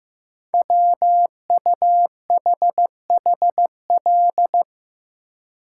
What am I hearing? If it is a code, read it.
Morse code, heard as a keyed tone, about 15 wpm, 700 Hz. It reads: WUHHL
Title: Code Groups - 5 Letters - 15wpm